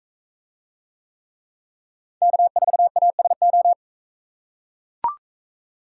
Morse code, heard as a keyed tone, about 41 wpm, 700 Hz. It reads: K4ASO